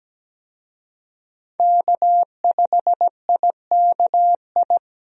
Morse code, heard as a keyed tone, about 17 wpm, 700 Hz. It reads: K5IKI
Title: USA Call Signs - 5 Characters - Repeated 3x - 17wpm